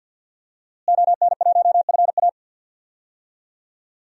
Morse code, heard as a keyed tone, about 52 wpm, 700 Hz. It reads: ON1FA
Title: DXCC-100 Call Signs - 5 Characters - Repeated 2x - 50wpm